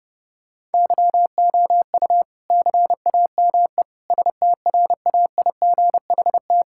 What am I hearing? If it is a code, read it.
Morse code, heard as a keyed tone, about 32 wpm, 700 Hz. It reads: YOU CAME STRAIGHT
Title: Sentences from Top 1000 Words - Review Part 6 - 30wpm